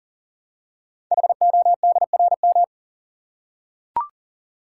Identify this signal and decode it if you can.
Morse code, heard as a keyed tone, about 40 wpm, 700 Hz. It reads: HODRM
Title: Code Groups - 5 Letters - 40wpm